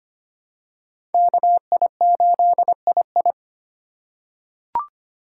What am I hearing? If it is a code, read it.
Morse code, heard as a keyed tone, about 25 wpm, 700 Hz. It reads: KI8II
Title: USA Call Signs - 3x Character Spacing - 25wpm